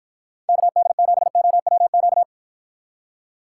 Morse code, heard as a keyed tone, about 52 wpm, 700 Hz. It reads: KD7OWQ